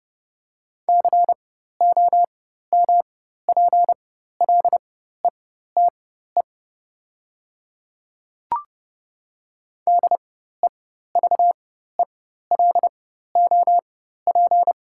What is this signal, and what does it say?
Morse code, heard as a keyed tone, about 30 wpm, spaced as for 14 wpm, 690 Hz. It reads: COMPLETE DEVELOP
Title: 4x Character Spacing - Top 500 Words - 30wpm